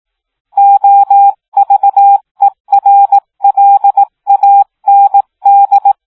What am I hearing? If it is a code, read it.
Morse code, heard as a keyed tone, about 18 wpm, 800 Hz. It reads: OVERLAND